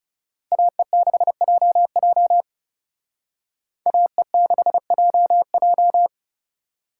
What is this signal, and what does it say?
Morse code, heard as a keyed tone, about 34 wpm, 700 Hz. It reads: AE6JJ AE6JJ
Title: Speed Racing - USA 2x2 Call Signs 30wpm